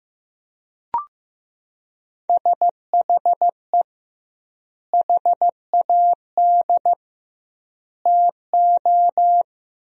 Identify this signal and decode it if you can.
Morse code, heard as a keyed tone, about 15 wpm, 700 Hz. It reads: SHE HAD TO
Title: 2x Word-Spacing - Sentences from Top 200 Words - 15wpm